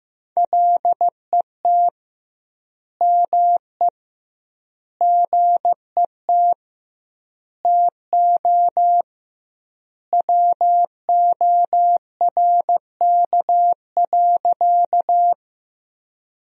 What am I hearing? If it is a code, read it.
Morse code, heard as a keyed tone, about 15 wpm, 700 Hz. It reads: LET ME GET TO WORK.
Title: 2x Word-Spacing - Sentences from Top 300 Words - 15wpm